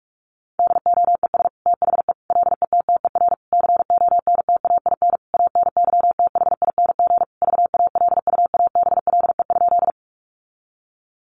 Morse code, heard as a keyed tone, about 44 wpm, 700 Hz. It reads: DOES THE LETTER CONTAIN ANYTHING VALUABLE?
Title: Sentences from Top 1100 Words 45wpm